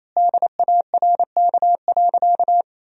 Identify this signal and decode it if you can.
Morse code, heard as a keyed tone, about 28 wpm, 700 Hz. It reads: DARK.
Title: 4x Word-Spacing - Sentences from Top 500 Words - 28wpm